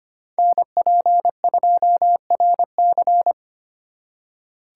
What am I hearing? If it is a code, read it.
Morse code, heard as a keyed tone, about 31 wpm, 700 Hz. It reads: NP2RC
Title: Speed Racing - USA 2x2 Call Signs 25wpm